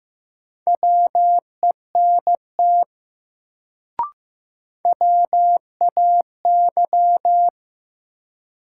Morse code, heard as a keyed tone, about 15 wpm, 700 Hz. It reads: WENT WAY
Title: Sets of 2 Words - New Word Drill - 15wpm